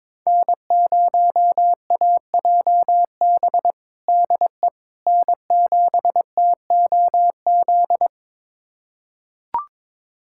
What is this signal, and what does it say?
Morse code, heard as a keyed tone, about 22 wpm, 700 Hz. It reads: N0AJB DE N7TOZ
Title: USA Callsign DE Callsign - 5 Characters - 22wpm